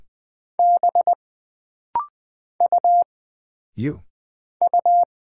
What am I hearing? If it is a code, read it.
Morse code, heard as a keyed tone, about 20 wpm, 700 Hz. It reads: B U U
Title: Single Character (Letter:Number:Punctuation:Prosign) - Rapid-Fire - 20wpm